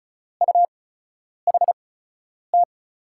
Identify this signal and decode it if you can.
Morse code, heard as a keyed tone, about 36 wpm, spaced as for 10 wpm, 700 Hz. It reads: UHT
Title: USA Call Signs - 6 Characters - 8x Character Spacing - 35wpm